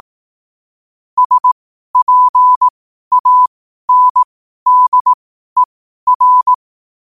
Morse code, heard as a keyed tone, about 18 wpm, spaced as for 12 wpm, 1000 Hz. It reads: SPANDER
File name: qotdfiles / 12wpm